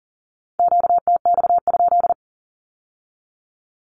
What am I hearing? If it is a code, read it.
Morse code, heard as a keyed tone, about 38 wpm, 700 Hz. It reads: QTX?
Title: Q-Codes with Spoken Meaning - 40wpm